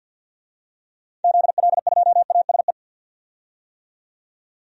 Morse code, heard as a keyed tone, about 45 wpm, 700 Hz. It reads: ZL2ASE
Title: DXCC-100 Call Signs - 6 Characters - 50wpm